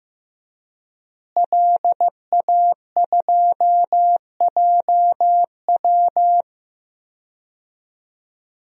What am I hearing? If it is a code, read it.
Morse code, heard as a keyed tone, about 15 wpm, 700 Hz. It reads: LA2JW
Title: DXCC-100 Call Signs - 5 Characters - Repeated 2x - 15wpm